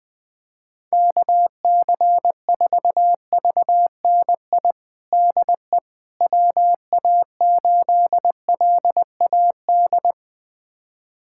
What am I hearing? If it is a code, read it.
Morse code, heard as a keyed tone, about 20 wpm, 700 Hz. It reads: KC4VNI DE WA8LAD
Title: USA Callsign DE Callsign 20wpm